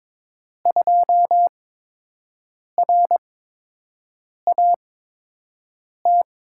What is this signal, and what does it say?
Morse code, heard as a keyed tone, about 22 wpm, spaced as for 6 wpm, 700 Hz. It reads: 2RAT
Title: USA Call Signs - 6 Characters - 8x Character Spacing - 22wpm